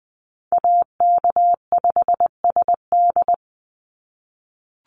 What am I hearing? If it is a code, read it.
Morse code, heard as a keyed tone, about 19 wpm, 700 Hz. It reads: AK5SD
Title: NA SOTA Chaser Callsigns 20wpm